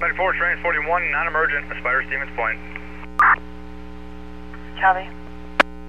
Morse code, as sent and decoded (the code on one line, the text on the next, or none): none